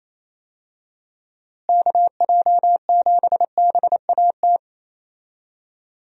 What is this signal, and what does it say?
Morse code, heard as a keyed tone, about 28 wpm, 700 Hz. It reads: KJ7BAT
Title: USA Call Signs - 6 Characters - Repeated 3x - 28wpm